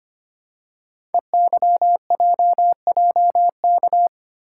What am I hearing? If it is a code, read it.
Morse code, heard as a keyed tone, about 25 wpm, 700 Hz. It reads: EYJJK
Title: Code Groups - 5 Letters - 25wpm